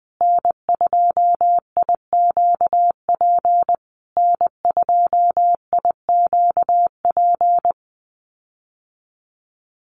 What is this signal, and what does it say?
Morse code, heard as a keyed tone, about 20 wpm, 700 Hz. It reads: N2IQP N2IQP
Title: USA Call Signs Repeated 3x 20wpm